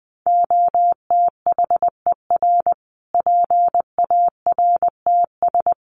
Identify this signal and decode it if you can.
Morse code, heard as a keyed tone, about 19 wpm, 700 Hz. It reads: OTHER PARTS
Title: Sets of 5 Words Encore 20wpm